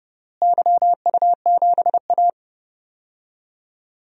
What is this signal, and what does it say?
Morse code, heard as a keyed tone, about 30 wpm, 700 Hz. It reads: YU7A
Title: DXCC-100 Call Signs - 4 Characters - Repeated 3x - 30wpm